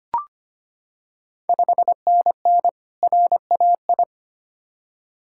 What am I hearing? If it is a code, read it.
Morse code, heard as a keyed tone, about 25 wpm, 690 Hz. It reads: 5NN RAI
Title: Ontario QSO Party Exchange 25wpm